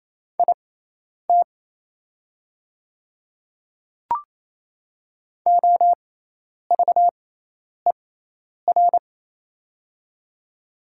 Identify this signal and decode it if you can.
Morse code, heard as a keyed tone, about 28 wpm, spaced as for 10 wpm, 700 Hz. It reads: IT OVER
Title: 6x Character Spacing - Top 200 Words - 28wpm